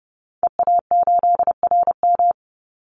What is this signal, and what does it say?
Morse code, heard as a keyed tone, about 30 wpm, 700 Hz. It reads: EA8RM
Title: Call Signs - 30wpm